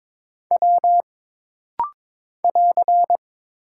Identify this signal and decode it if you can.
Morse code, heard as a keyed tone, about 22 wpm, 700 Hz. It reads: W <AR>
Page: This rapid-fire timing is geared to push your ICR (Instant Character Recognition) skills.